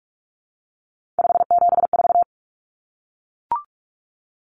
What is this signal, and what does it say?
Morse code, heard as a keyed tone, about 44 wpm, 700 Hz. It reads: 574
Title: Numbers - 3-Digit 45wpm